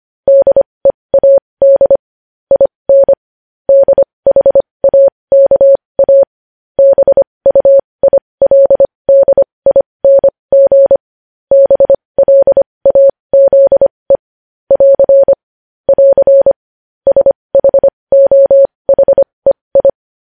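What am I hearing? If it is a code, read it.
Morse code, heard as a keyed tone, about 25 wpm, spaced as for 20 wpm, 550 Hz. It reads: DEAD IN DHAKA BUILDING BLAZE <AR> <AR> SHOHEI
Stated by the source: Morse code news headlines sent at 20 wpm.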